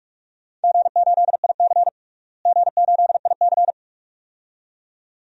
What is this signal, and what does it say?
Morse code, heard as a keyed tone, about 46 wpm, 700 Hz. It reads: G8IC G8IC
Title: DXCC-100 Call Signs - 4 Characters - Repeated 2x - 45wpm